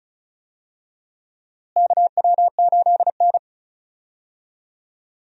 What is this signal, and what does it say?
Morse code, heard as a keyed tone, about 36 wpm, 700 Hz. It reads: KW8N
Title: USA Call Signs - 4 Characters - Repeated 2x - 35wpm